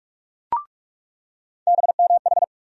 Morse code, heard as a keyed tone, about 44 wpm, 700 Hz. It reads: BMH
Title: USA Call Sign Postfixes - 3 Letters 45wpm